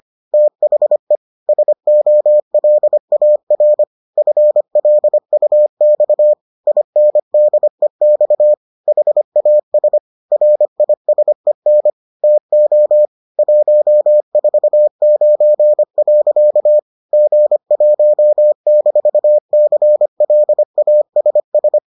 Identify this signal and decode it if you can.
Morse code, heard as a keyed tone, about 25 wpm, 600 Hz. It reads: THE SOLAR FLUX INDEX HAS RISEN TO 149. G1-CLASS